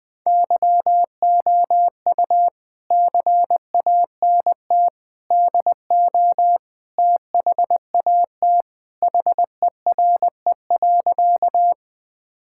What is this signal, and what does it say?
Morse code, heard as a keyed tone, about 20 wpm, 700 Hz. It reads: YOU CANT DO THAT HERE.
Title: Sentences with 5-Word Phrases - Spoken Beginning - 20wpm